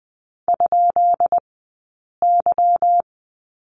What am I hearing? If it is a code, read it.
Morse code, heard as a keyed tone, about 20 wpm, spaced as for 8 wpm, 700 Hz. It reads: ?Y
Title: 151 Lesson 25d - Letter p - Two Characters 20wpm